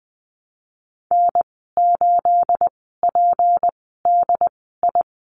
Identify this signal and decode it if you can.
Morse code, heard as a keyed tone, about 20 wpm, spaced as for 15 wpm, 700 Hz. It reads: N8PDI
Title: 238 Lesson 38g - Letter q - Call Signs 20wpm